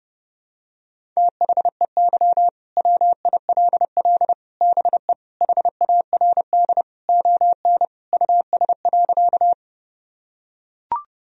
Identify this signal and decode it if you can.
Morse code, heard as a keyed tone, about 30 wpm, 700 Hz. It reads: THEY WILL BE HARD ON US.
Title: Sentences from Top 1000 Words - Review Part 1 - 30wpm